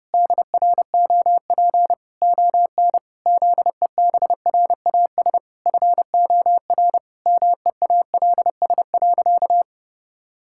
Morse code, heard as a keyed tone, about 30 wpm, 700 Hz. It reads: DROP ON ZEBRAS FOR MEALS.
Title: Pangram Challenge v2 - 30wpm